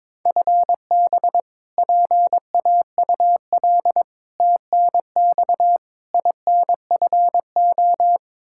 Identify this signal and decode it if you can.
Morse code, heard as a keyed tone, about 21 wpm, 700 Hz. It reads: FB PAUL TNX INFO
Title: Practice QSO - Exchange 3 - More Info - 22wpm